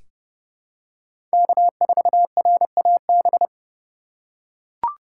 Morse code, heard as a keyed tone, about 30 wpm, 700 Hz. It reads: K4RAB
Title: K1USN SST QSO Practice - 30wpm